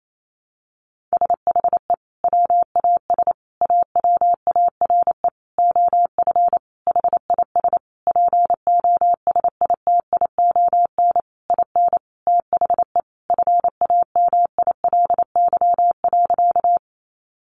Morse code, heard as a keyed tone, about 28 wpm, 700 Hz. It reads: SHE WAS AWARE OF HIS POSITION IN THE FAMILY.
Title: Sentences from Top 1200 Words 28wpm